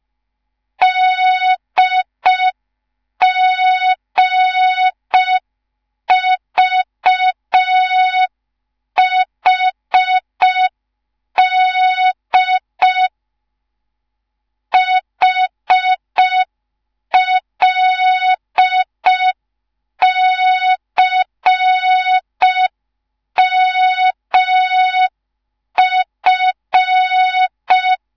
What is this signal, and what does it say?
Morse code, heard as a keyed tone, about 5 wpm, 3700 Hz. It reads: DGVHD HLCMF
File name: Morze-HA2MN / hang / 10GWZ HA2MN